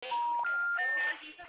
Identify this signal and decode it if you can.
Special information tone, with speech.